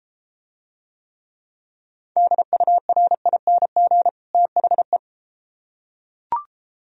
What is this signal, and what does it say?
Morse code, heard as a keyed tone, about 31 wpm, 700 Hz. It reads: DURING THE